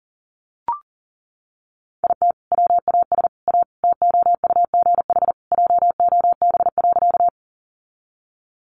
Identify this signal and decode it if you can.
Morse code, heard as a keyed tone, about 40 wpm, 700 Hz. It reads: IT WAS A TOUGH JOB.
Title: Sentences from Top 1200 Words 40wpm